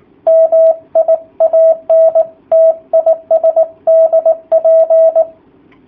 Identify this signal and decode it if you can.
Morse code, heard as a keyed tone, about 18 wpm, 650 Hz. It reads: MI ANT IS DP
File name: jh8cbh / 2019cw / sp18- w